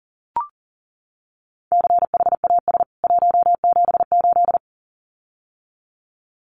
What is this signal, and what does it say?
Morse code, heard as a keyed tone, about 40 wpm, 700 Hz. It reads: CHAS 178
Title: CWT Exchange 40wpm